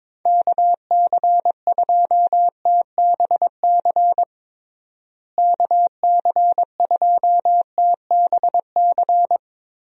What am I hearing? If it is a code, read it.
Morse code, heard as a keyed tone, about 22 wpm, 700 Hz. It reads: KC2TBC KC2TBC
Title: USA Call Signs - 6 Characters - Repeated 3x - 22wpm